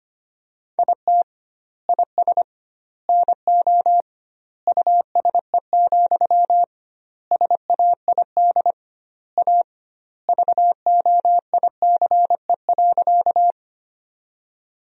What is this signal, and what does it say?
Morse code, heard as a keyed tone, about 25 wpm, 700 Hz. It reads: IT IS NO USE, SAID A VOICE.
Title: 2x Word-Spacing - Sentences from Top 500 Words - 25wpm